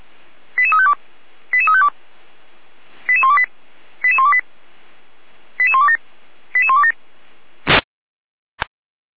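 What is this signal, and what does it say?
Five-tone selective call (ZVEI), heard as quick runs of five tones, tones of about 70 ms. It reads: 88352 88352 88128 88128 88127 88127
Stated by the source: ZVEI five-tone selective call received via SDR.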